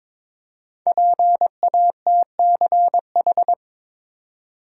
Morse code, heard as a keyed tone, about 22 wpm, 700 Hz. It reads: PATCH
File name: 5 Letter Words - First 3 Letters Given - 22wpm